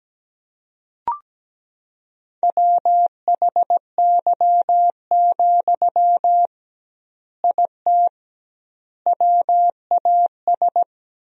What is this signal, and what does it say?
Morse code, heard as a keyed tone, about 17 wpm, 700 Hz. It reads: WHY, IT WAS
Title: 2x Word-Spacing - Sentences from Top 300 Words - 17wpm